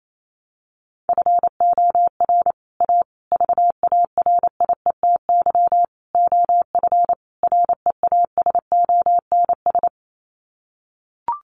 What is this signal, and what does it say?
Morse code, heard as a keyed tone, about 28 wpm, 700 Hz. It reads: FOR A VARIETY OF REASONS
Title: Sets of 5 Words Encore 28wpm